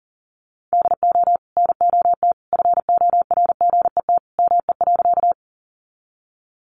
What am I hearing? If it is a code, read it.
Morse code, heard as a keyed tone, about 40 wpm, 700 Hz. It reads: DO NOT FORGET ME.
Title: Sentences from Top 1200 Words 40wpm